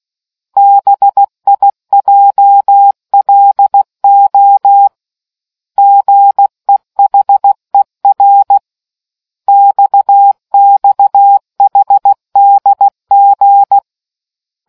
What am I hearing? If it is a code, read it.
Morse code, heard as a keyed tone, about 16 wpm, 800 Hz. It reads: BIJLO GEHER XXHDG